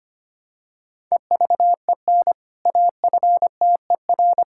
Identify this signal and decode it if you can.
Morse code, heard as a keyed tone, about 25 wpm, 700 Hz. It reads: EVEN AFTER
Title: Sentences from Top 1000 Words - Review Part 5 - 25wpm